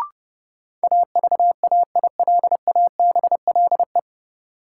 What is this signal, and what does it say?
Morse code, heard as a keyed tone, about 32 wpm, 700 Hz. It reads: AVAILABLE
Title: Sentences from Top 600 Words - New Word Drill - 30wpm